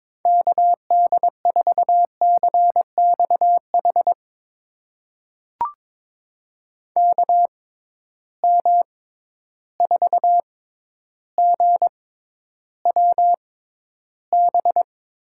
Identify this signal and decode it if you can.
Morse code, heard as a keyed tone, about 22 wpm, spaced as for 11 wpm, 700 Hz. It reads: KD4CXH KM4GWB
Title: USA Call Signs - 6 Characters - 6x Character Spacing - 22wpm